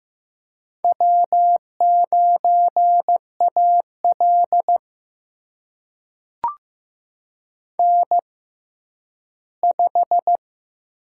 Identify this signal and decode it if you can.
Morse code, heard as a keyed tone, about 15 wpm, spaced as for 10 wpm, 700 Hz. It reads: W9AL N5
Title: USA Call Signs - 4 Characters - 6x Character Spacing - 15wpm